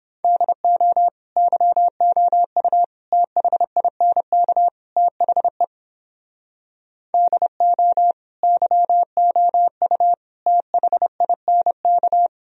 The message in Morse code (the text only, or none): DO YOU THINK THE DO YOU THINK